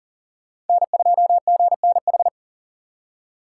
Morse code, heard as a keyed tone, about 48 wpm, 700 Hz. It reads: N2GNH
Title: Speed Racing - USA 1x3 Call Signs 40wpm